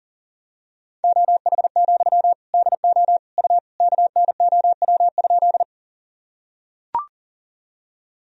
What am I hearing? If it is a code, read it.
Morse code, heard as a keyed tone, about 40 wpm, 700 Hz. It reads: OH, DO U KNOW?
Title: Sentences from Top 500 Words and Abbreviations 40wpm